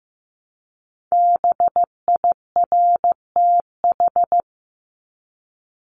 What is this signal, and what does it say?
Morse code, heard as a keyed tone, about 15 wpm, 700 Hz. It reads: BIRTH
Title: Top 751-1000 Words 15wpm